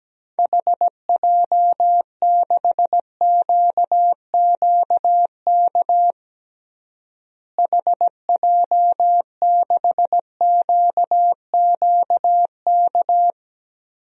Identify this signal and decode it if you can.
Morse code, heard as a keyed tone, about 17 wpm, 700 Hz. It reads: HJ6QQK HJ6QQK
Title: DXCC-100 Call Signs - 6 Characters - Repeated 3x - 17wpm